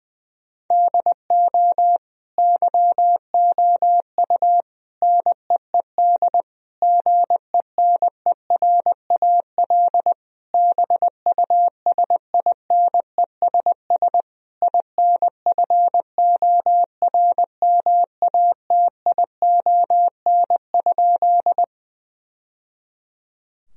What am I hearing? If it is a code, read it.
Morse code, heard as a keyed tone, about 21 wpm, 700 Hz. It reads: DO YOU NEED GENERAL BUSINESS INFORMATION?
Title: Sentences from Top 1000 Words - Review Part 3 - 20wpm